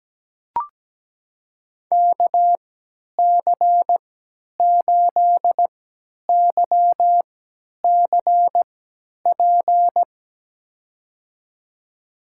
Morse code, heard as a keyed tone, about 17 wpm, spaced as for 10 wpm, 700 Hz. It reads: KC8YCP
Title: USA Call Signs - 6 Characters - 3x Character Spacing - 17wpm